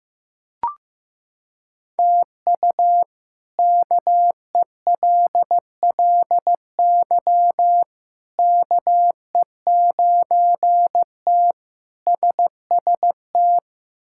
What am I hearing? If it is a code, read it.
Morse code, heard as a keyed tone, about 15 wpm, 700 Hz. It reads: TU KELLY KE9T SST
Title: K1USN SST QSO Practice - 15wpm